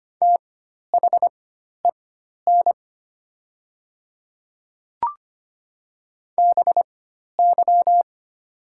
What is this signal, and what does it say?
Morse code, heard as a keyed tone, about 25 wpm, spaced as for 12 wpm, 700 Hz. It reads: THEN BY
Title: 4x Character Spacing - Top 200 Words - 25wpm